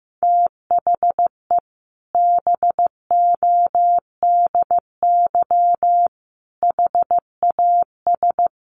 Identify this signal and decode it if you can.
Morse code, heard as a keyed tone, about 14 wpm, 700 Hz. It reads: THE BODY HAS
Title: A Sound Of Thunder 15wpm